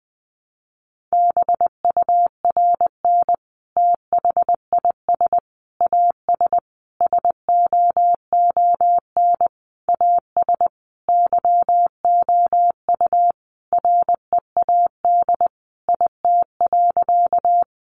Morse code, heard as a keyed tone, about 20 wpm, 700 Hz. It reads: BURN THIS AS SOON AS YOU READ IT.
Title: Sentences from Top 1300 Words 20wpm